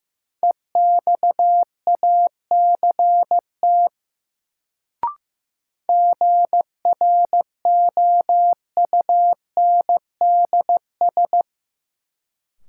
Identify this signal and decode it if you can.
Morse code, heard as a keyed tone, about 15 wpm, 700 Hz. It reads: EXACT GROUNDS
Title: Sentences from Top 1300 Words - New Word Drill - 15wpm